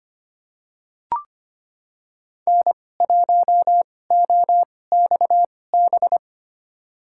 Morse code, heard as a keyed tone, about 25 wpm, spaced as for 18 wpm, 700 Hz. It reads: N1OXB
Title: USA Call Signs - 5 Characters - 2x Character Spacing - 25wpm